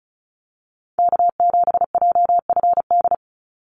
Morse code, heard as a keyed tone, about 34 wpm, 700 Hz. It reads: K7JFD
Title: NA SOTA Chaser Callsigns 35wpm